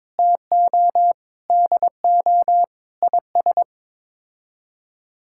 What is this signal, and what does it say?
Morse code, heard as a keyed tone, about 22 wpm, 700 Hz. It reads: TO DO IS
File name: practice / 22wpm / Sets of 6 Words - 22wpm